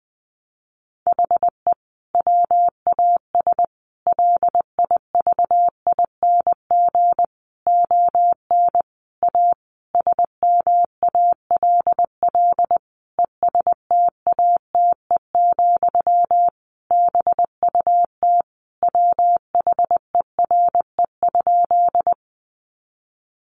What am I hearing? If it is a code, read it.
Morse code, heard as a keyed tone, about 20 wpm, 700 Hz. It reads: HE WAS LIVING ON A SMALL ESTATE, BUT WHERE?
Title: Sentences from Top 1200 Words 20wpm